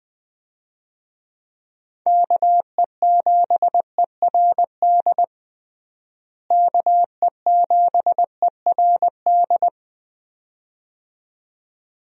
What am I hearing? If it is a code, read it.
Morse code, heard as a keyed tone, about 20 wpm, 700 Hz. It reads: KE7ERD KE7ERD
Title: USA Call Signs - 6 Characters - Repeated 2x - 20wpm